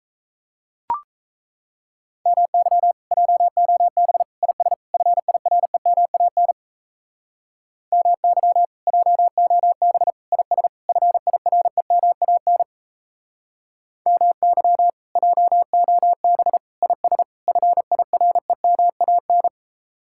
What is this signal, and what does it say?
Morse code, heard as a keyed tone, about 35 wpm, 700 Hz. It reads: MY JOB IS FIREMAN MY JOB IS FIREMAN MY JOB IS FIREMAN
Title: Speed-Racing Common QSO Phrases - 28wpm